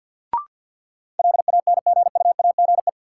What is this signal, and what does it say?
Morse code, heard as a keyed tone, about 45 wpm, 700 Hz. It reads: LANGUAGE
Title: Sentences from Top 500 Words - New Word Drill - 50wpm